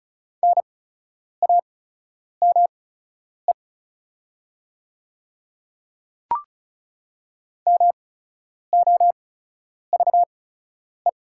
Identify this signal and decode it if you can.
Morse code, heard as a keyed tone, about 34 wpm, spaced as for 10 wpm, 700 Hz. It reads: NAME MOVE
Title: 8x Character Spacing - Top 300 Words - 35wpm